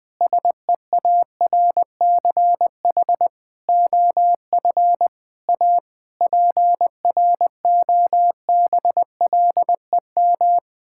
Morse code, heard as a keyed tone, about 20 wpm, 700 Hz. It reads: SEARCH OF A PROBLEM
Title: Sets of 8 Words - 20wpm